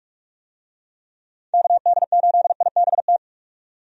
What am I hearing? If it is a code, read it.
Morse code, heard as a keyed tone, about 43 wpm, 700 Hz. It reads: KD8IBT
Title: USA Call Signs - 6 Characters - 45wpm